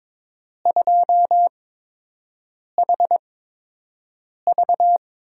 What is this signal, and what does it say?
Morse code, heard as a keyed tone, about 22 wpm, spaced as for 6 wpm, 700 Hz. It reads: 2HV
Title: USA Call Signs - 6 Characters - 8x Character Spacing - 22wpm